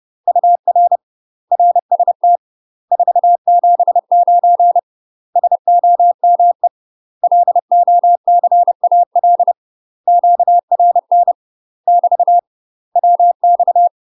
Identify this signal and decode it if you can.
Morse code, heard as a keyed tone, about 30 wpm, 700 Hz. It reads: UR RST 479 SOME LOCAL QRN <BT> WX
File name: UK MASTER QSOs TRAINING / 30WPM / UK MASTER QSOs 0027